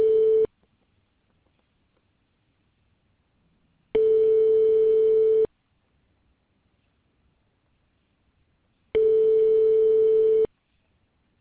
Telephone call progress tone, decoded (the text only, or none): ringback tone